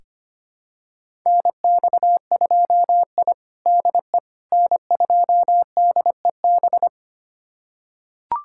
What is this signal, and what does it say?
Morse code, heard as a keyed tone, about 25 wpm, 700 Hz. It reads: NX2I DE N2DEB
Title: USA Callsign DE Callsign 25wpm